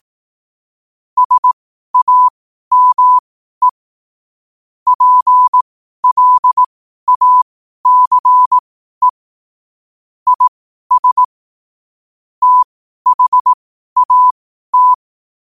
Morse code, heard as a keyed tone, about 18 wpm, spaced as for 12 wpm, 1000 Hz. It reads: SAME PLACE IS THAT